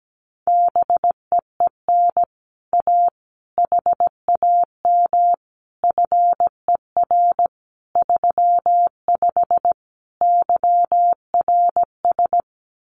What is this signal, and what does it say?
Morse code, heard as a keyed tone, about 17 wpm, 700 Hz. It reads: BEEN A HAM FER 35 YRS
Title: Common QSO Phrases Encore 17wpm